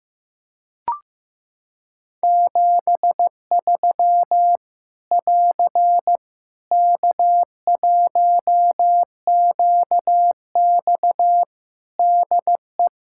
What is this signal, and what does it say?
Morse code, heard as a keyed tone, about 14 wpm, 700 Hz. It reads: 73 <AR> K1QX DE
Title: Practice QSO - Exchange 4 - Ending - 15wpm